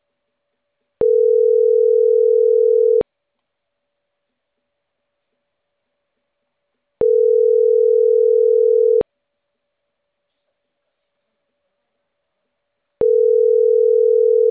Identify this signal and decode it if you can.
Ringback tone (North America), 440 + 480 Hz.